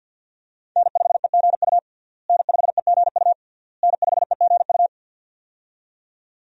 Morse code, heard as a keyed tone, about 49 wpm, 690 Hz. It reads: N5EGU N5EGU N5EGU
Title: USA Call Signs - 5 Characters - Repeated 3x - 50wpm